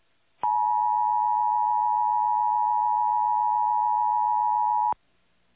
Emergency Alert System (SAME) tones (the 853 and 960 Hz attention tone). It is heard from 0.4 to 5.0 s.
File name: eas / KUTTKGMT / KUTTKGMT